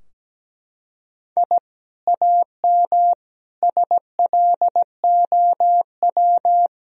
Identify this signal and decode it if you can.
Morse code, heard as a keyed tone, about 17 wpm, 700 Hz. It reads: I AM SLOW
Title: Sentences from Top 1000 Words - Review Part 2 - 17wpm